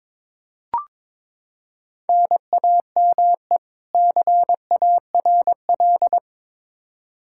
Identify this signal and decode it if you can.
Morse code, heard as a keyed tone, about 22 wpm, 700 Hz. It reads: NAME CARL
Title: Practice QSO - Exchange 1 - Beginning - 22wpm